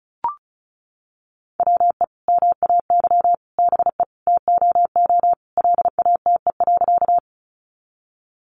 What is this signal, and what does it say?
Morse code, heard as a keyed tone, about 34 wpm, 700 Hz. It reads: WE MAY BE TOO LATE.
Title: Sentences from Top 300 Words 35wpm